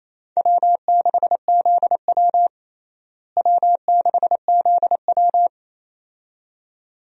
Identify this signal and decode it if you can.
Morse code, heard as a keyed tone, about 28 wpm, 700 Hz. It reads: W6ZW W6ZW
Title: USA Call Signs - 4 Characters - Repeated 2x - 28wpm